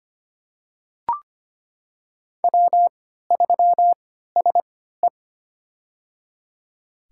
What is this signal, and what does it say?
Morse code, heard as a keyed tone, about 25 wpm, spaced as for 14 wpm, 700 Hz. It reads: W3SE
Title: USA Call Signs - 4 Characters - 3x Character Spacing - 25wpm